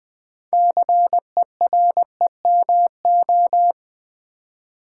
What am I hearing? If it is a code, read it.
Morse code, heard as a keyed tone, about 20 wpm, 700 Hz. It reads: CEREMO
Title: Build A Word - 2 Letters At A Time - 8-Letter Words - 20wpm